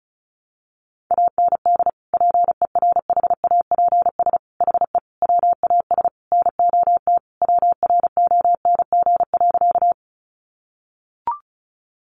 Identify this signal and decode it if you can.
Morse code, heard as a keyed tone, about 34 wpm, 700 Hz. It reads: AND PERHAPS HE WAS NOT WRONG.
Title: Sentences from Top 900 Words 35wpm